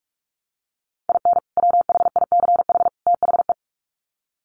Morse code, heard as a keyed tone, about 44 wpm, 700 Hz. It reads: IN WHICH THE
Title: Sets of 3 Words 45wpm